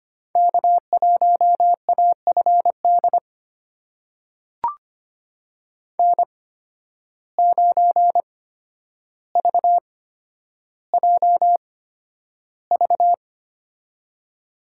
Morse code, heard as a keyed tone, about 25 wpm, spaced as for 11 wpm, 700 Hz. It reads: K1AFD N9VJV
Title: USA Call Signs - 5 Characters - 8x Character Spacing - 25wpm